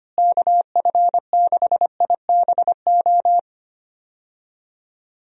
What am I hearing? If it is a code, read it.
Morse code, heard as a keyed tone, about 25 wpm, 700 Hz. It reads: KF6IBO
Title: USA Call Signs - 6 Characters - Repeated 2x - 25wpm